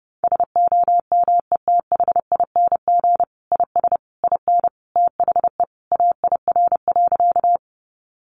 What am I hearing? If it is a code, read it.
Morse code, heard as a keyed tone, about 30 wpm, 700 Hz. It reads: SOMETHING IS IN THE AIR.
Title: Sentences from Top 600 Words 30wpm